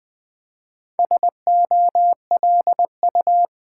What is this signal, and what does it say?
Morse code, heard as a keyed tone, about 20 wpm, 700 Hz. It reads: SOLU
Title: Build A Word - 2 Letters At A Time - 8-Letter Words - 20wpm